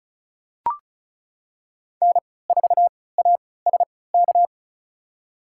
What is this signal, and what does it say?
Morse code, heard as a keyed tone, about 36 wpm, spaced as for 20 wpm, 700 Hz. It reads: N4ASK
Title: USA Call Signs - 5 Characters - 3x Character Spacing - 35wpm